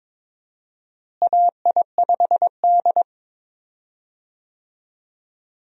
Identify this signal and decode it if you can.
Morse code, heard as a keyed tone, about 22 wpm, 700 Hz. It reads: AI5D
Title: USA Call Signs - 4 Characters - Repeated 3x - 22wpm